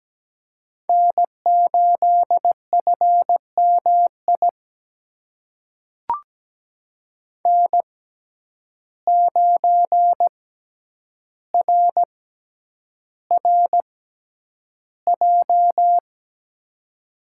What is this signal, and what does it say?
Morse code, heard as a keyed tone, about 17 wpm, spaced as for 9 wpm, 700 Hz. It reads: N8FMI N9RRJ
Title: USA Call Signs - 5 Characters - 6x Character Spacing - 17wpm